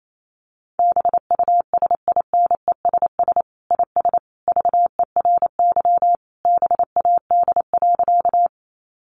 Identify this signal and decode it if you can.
Morse code, heard as a keyed tone, about 28 wpm, 700 Hz. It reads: BUSINESS IS VERY BAD.
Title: Sentences from Top 700 Words 28wpm